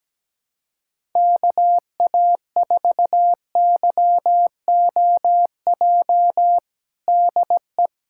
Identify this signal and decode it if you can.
Morse code, heard as a keyed tone, about 17 wpm, 700 Hz. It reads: KA4YOJ DE
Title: USA Callsign DE Callsign 17wpm